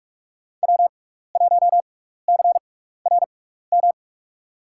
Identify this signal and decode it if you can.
Morse code, heard as a keyed tone, about 46 wpm, spaced as for 16 wpm, 700 Hz. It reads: W1CRM
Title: USA Call Signs - 6x Character Spacing - 45wpm